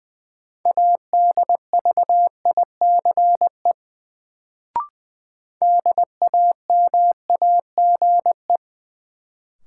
Morse code, heard as a keyed tone, about 20 wpm, 700 Hz. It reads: ADVICE DAMAGE
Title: Sentences from Top 1100 Words - New Word Drill Alt - 20wpm